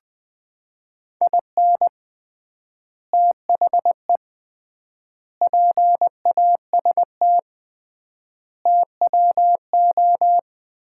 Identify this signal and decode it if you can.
Morse code, heard as a keyed tone, about 21 wpm, 700 Hz. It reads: IN THE PAST TWO
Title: 3x Word-Spacing - Sets of 5 Words - 20wpm